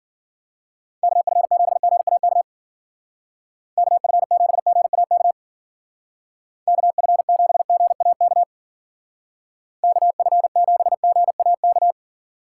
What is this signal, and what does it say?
Morse code, heard as a keyed tone, about 52 wpm, 700 Hz. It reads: KF7GAK KF7GAK KF7GAK KF7GAK